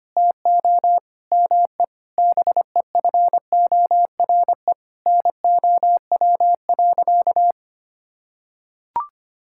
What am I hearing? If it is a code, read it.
Morse code, heard as a keyed tone, about 25 wpm, 700 Hz. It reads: TO ME BEFORE NOW.
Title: Sentences from Top 1000 Words - Review Part 6 - 25wpm